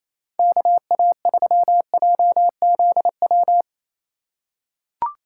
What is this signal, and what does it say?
Morse code, heard as a keyed tone, about 28 wpm, 700 Hz. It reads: KA3JZW
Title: USA Call Signs - 6 Characters - 3x Character Spacing - 28wpm